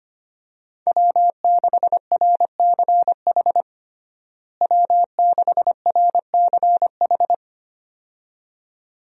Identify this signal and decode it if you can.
Morse code, heard as a keyed tone, about 25 wpm, 700 Hz. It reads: W6RCH W6RCH
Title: USA Call Signs - 5 Characters - Repeated 2x - 25wpm